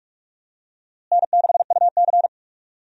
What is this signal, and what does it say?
Morse code, heard as a keyed tone, about 41 wpm, 700 Hz. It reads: N6UC
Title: USA Call Signs - 4 Characters - Repeated 3x - 45wpm